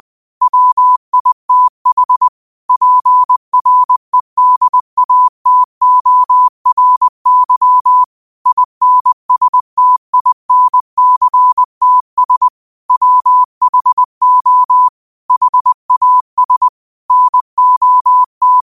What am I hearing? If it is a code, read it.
Morse code, heard as a keyed tone, about 20 wpm, 1000 Hz. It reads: WITH PREDATORY INSTINCTS WHO HAS NOT
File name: qotdfiles / 20wpm